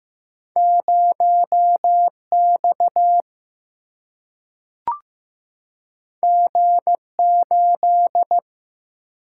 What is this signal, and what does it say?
Morse code, heard as a keyed tone, about 15 wpm, 700 Hz. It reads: 0X G8
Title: Single-Letter Single-Number 15wpm